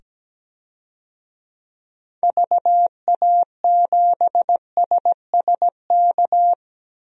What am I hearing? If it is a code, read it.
Morse code, heard as a keyed tone, about 17 wpm, 700 Hz. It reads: VA7SSK
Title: DXCC-100 Call Signs - 6 Characters - 17wpm